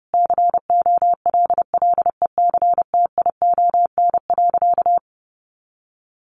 Morse code, heard as a keyed tone, about 30 wpm, 700 Hz. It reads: COLLECTION.
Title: Sentences from Top 1200 Words 30wpm